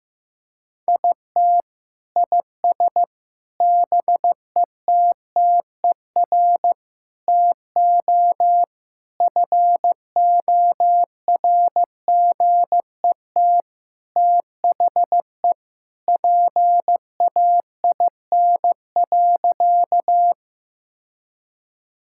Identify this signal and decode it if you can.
Morse code, heard as a keyed tone, about 15 wpm, 700 Hz. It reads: IT IS BETTER TO FORGET THE PAIN.
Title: Sentences from Top 1600 Words 15wpm